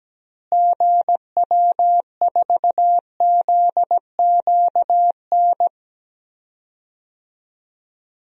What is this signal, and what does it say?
Morse code, heard as a keyed tone, about 17 wpm, 700 Hz. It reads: GW4ZQN
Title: DXCC-100 Call Signs - 6 Characters - Repeated 3x - 17wpm